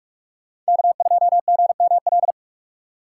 Morse code, heard as a keyed tone, about 62 wpm, spaced as for 54 wpm, 700 Hz. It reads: K2GML
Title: Speed Racing - USA 1x3 Call Signs 45wpm